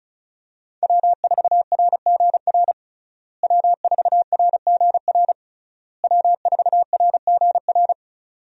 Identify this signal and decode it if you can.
Morse code, heard as a keyed tone, about 36 wpm, 700 Hz. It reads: W4RGR W4RGR W4RGR
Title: USA Call Signs - 5 Characters - Repeated 3x - 35wpm